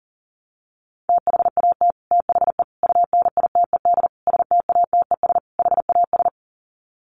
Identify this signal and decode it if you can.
Morse code, heard as a keyed tone, about 37 wpm, 700 Hz. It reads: THAT THE UNITED STATES HAS
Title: Sets of 5 Words Encore 40wpm